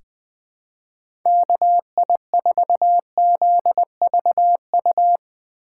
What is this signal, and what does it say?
Morse code, heard as a keyed tone, about 20 wpm, 700 Hz. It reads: KI4ZVU
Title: K1USN SST QSO Practice - 20wpm